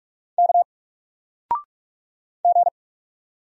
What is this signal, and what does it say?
Morse code, heard as a keyed tone, about 47 wpm, 700 Hz. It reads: K G
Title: Single Character (Letter:Number:Punctuation:Prosign) 45wpm